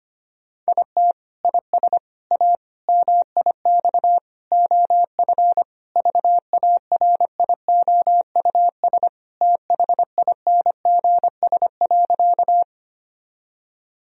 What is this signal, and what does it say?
Morse code, heard as a keyed tone, about 25 wpm, 700 Hz. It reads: IT IS A MIX OF VARIOUS THINGS.
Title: Sentences from Top 1000 Words - Review Part 4 - 25wpm